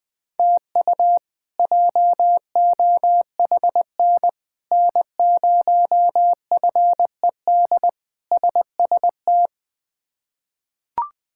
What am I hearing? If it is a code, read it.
Morse code, heard as a keyed tone, about 20 wpm, 700 Hz. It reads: TU JOHN N0FED SST
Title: K1USN SST QSO Practice - 20wpm